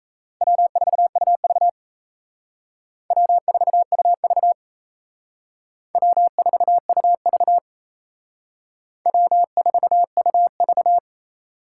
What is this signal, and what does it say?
Morse code, heard as a keyed tone, about 34 wpm, 700 Hz. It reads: W4UV W4UV W4UV W4UV
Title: Speed Racing - USA 1x2 Call Signs 28wpm